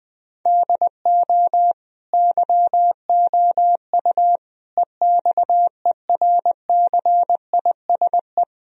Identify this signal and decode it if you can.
Morse code, heard as a keyed tone, about 20 wpm, 700 Hz. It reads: DO YOU EXERCISE
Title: Sentences from Top 1000 Words - Review Part 4 - 20wpm